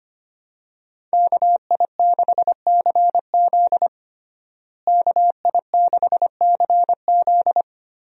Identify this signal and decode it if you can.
Morse code, heard as a keyed tone, about 25 wpm, 700 Hz. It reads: KI6CZ KI6CZ
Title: USA Call Signs - 5 Characters - Repeated 2x - 25wpm